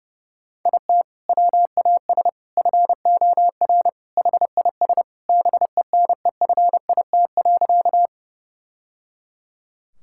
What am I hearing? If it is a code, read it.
Morse code, heard as a keyed tone, about 32 wpm, 700 Hz. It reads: IT WAS FOR HIS BENEFIT.
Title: Sentences from Top 1000 Words - Review Part 6 - 30wpm